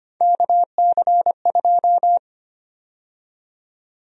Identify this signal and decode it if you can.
Morse code, heard as a keyed tone, about 25 wpm, 700 Hz. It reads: KC2
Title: USA Call Sign Prefixes - 25wpm